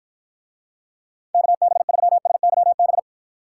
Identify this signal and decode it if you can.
Morse code, heard as a keyed tone, about 53 wpm, 700 Hz. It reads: KB3SYB